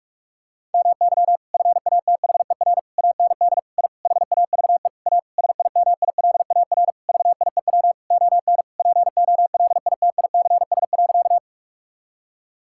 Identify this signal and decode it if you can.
Morse code, heard as a keyed tone, about 41 wpm, 700 Hz. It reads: MY FATHER AND I HAVE A SIMILAR VIEW ON POLITICS.
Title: Sentences from Top 1000 Words - Review Part 6 - 45wpm